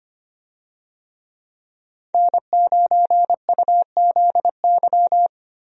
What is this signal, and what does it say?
Morse code, heard as a keyed tone, about 25 wpm, 700 Hz. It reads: N9UZY
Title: USA Call Signs - 5 Characters - Repeated 3x - 25wpm